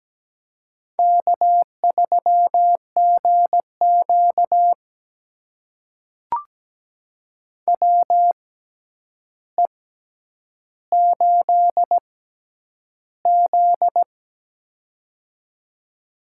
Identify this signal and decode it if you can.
Morse code, heard as a keyed tone, about 17 wpm, spaced as for 9 wpm, 700 Hz. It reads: K3GQ WE8Z
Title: USA Call Signs - 6x Character Spacing - 17wpm